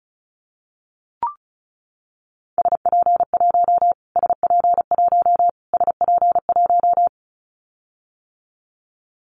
Morse code, heard as a keyed tone, about 34 wpm, 700 Hz. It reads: SP1 SP1 SP1
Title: Callsign Prefixes for Top 25 DXCC - Repeated 3x - 35wpm